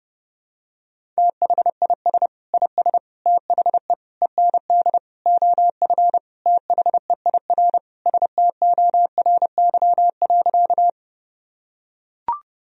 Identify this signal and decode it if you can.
Morse code, heard as a keyed tone, about 30 wpm, 700 Hz. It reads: THIS IS THE END OF THEIR STORY.
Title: Sentences from Top 1000 Words - Review Part 1 - 30wpm